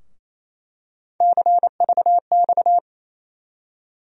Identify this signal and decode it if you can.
Morse code, heard as a keyed tone, about 28 wpm, 700 Hz. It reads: CVX
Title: USA Call Sign Postfixes - 3 Letters 28wpm